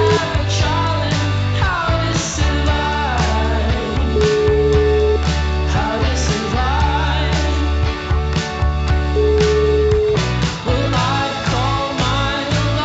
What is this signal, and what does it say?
Ringback tone (Europe, 425 Hz), with noise.